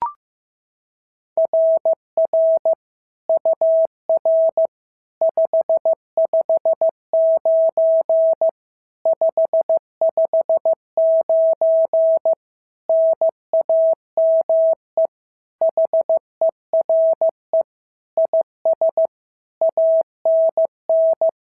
Morse code, heard as a keyed tone, about 14 wpm, 650 Hz. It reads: RR UR 559 559 NAME HERE IS ANN
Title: SOTA QSO Practice 15wpm